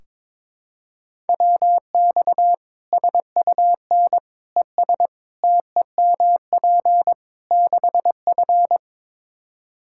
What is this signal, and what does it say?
Morse code, heard as a keyed tone, about 22 wpm, 700 Hz. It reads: WX SUN ES TEMP 6F
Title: Practice QSO - Exchange 2 - Info - 22wpm